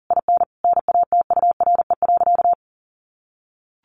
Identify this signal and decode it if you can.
Morse code, heard as a keyed tone, about 37 wpm, 700 Hz. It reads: IN NATURE.
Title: Sentences from Top 1400 Words 40wpm